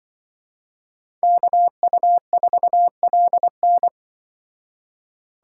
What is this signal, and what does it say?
Morse code, heard as a keyed tone, about 24 wpm, 700 Hz. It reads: KU4LN
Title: Speed Racing - USA 2x2 Call Signs 20wpm